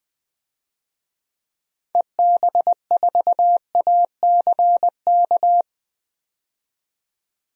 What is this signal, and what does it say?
Morse code, heard as a keyed tone, about 20 wpm, 700 Hz. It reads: EB4ACK
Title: DXCC-100 Call Signs - 6 Characters - 20wpm